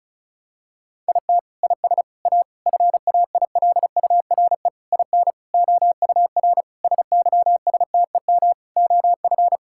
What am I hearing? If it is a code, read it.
Morse code, heard as a keyed tone, about 36 wpm, 700 Hz. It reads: IT IS A FAILURE IN OUR SYSTEM OF
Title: Sentences from Top 1000 Words - Review Part 6 - 35wpm